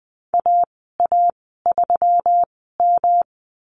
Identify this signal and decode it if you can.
Morse code, heard as a keyed tone, about 20 wpm, spaced as for 15 wpm, 700 Hz. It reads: AA3M
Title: 161 Lesson 26g - Letter g - Call Signs 20wpm